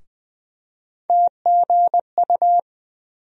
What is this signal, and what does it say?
Morse code, heard as a keyed tone, about 20 wpm, 700 Hz. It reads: TGU
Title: USA Call Sign Postfixes - 3 Letters 20wpm